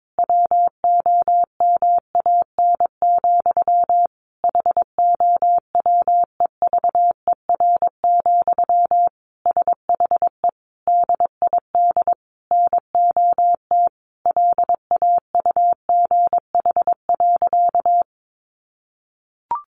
Morse code, heard as a keyed tone, about 22 wpm, 700 Hz. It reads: WOMAN, HOWEVER, SHE DID NOT LAUGH.
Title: Sentences from Top 1100 Words 22wpm